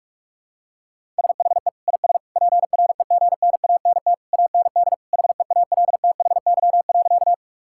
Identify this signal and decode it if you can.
Morse code, heard as a keyed tone, about 46 wpm, 700 Hz. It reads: SHE IS PREGNANT AND HEALTHY.
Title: Sentences from Top 1000 Words - Review Part 6 - 45wpm